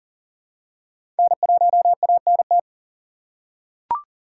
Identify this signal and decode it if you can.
Morse code, heard as a keyed tone, about 40 wpm, 700 Hz. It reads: N1ANT
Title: USA Call Signs - 5 Characters - 2x Character Spacing - 40wpm